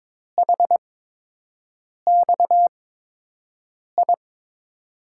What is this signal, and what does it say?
Morse code, heard as a keyed tone, about 22 wpm, spaced as for 6 wpm, 700 Hz. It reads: HXI
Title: USA Call Signs - 8x Character Spacing - 22wpm